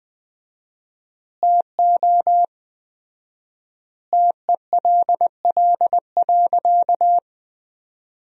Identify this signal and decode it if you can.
Morse code, heard as a keyed tone, about 20 wpm, 700 Hz. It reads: TO TELL.
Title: 4x Word-Spacing - Sentences from Top 200 Words - 20wpm